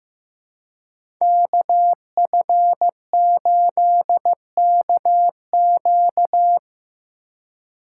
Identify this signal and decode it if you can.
Morse code, heard as a keyed tone, about 15 wpm, 700 Hz. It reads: KF8KQ
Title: USA Call Signs - 5 Characters - 2x Character Spacing - 15wpm